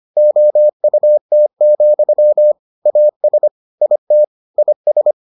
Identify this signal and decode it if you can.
Morse code, heard as a keyed tone, about 25 wpm, 600 Hz. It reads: OUT, AS IT IS